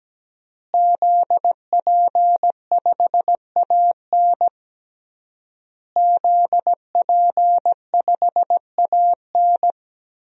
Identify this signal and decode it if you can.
Morse code, heard as a keyed tone, about 17 wpm, 700 Hz. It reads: ZP5AN ZP5AN
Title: DXCC-100 Call Signs - 5 Characters - Repeated 3x - 17wpm